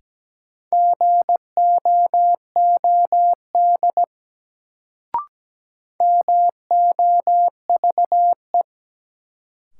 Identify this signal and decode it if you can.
Morse code, heard as a keyed tone, about 17 wpm, 700 Hz. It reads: GOOD MOVE
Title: Sentences from Top 200 Words - New Word Drill - 17wpm